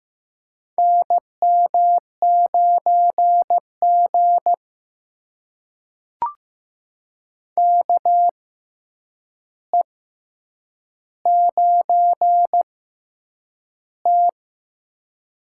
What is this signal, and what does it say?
Morse code, heard as a keyed tone, about 15 wpm, spaced as for 8 wpm, 700 Hz. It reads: NM9G KE9T
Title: USA Call Signs - 6x Character Spacing - 15wpm